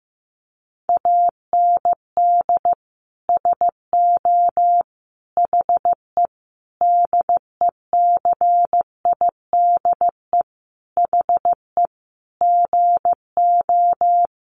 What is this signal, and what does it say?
Morse code, heard as a keyed tone, about 15 wpm, 700 Hz. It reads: AND SO HE DECIDE HE GO
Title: Sentences from Top 500 Words 15wpm